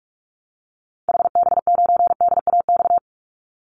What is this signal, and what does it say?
Morse code, heard as a keyed tone, about 44 wpm, 700 Hz. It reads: HB9DAX
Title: Call Signs Repeated 2x - 45wpm